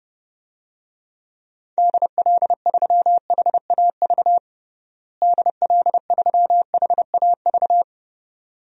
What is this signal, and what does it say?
Morse code, heard as a keyed tone, about 30 wpm, 700 Hz. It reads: DL3HAV DL3HAV
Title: DXCC-100 Call Signs - 6 Characters - Repeated 2x - 30wpm